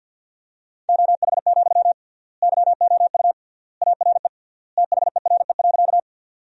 Morse code, heard as a keyed tone, about 50 wpm, 700 Hz. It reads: OH, YOU ARE THERE.
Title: 3x Word-Spacing - Sentences from Top 500 Words - 50wpm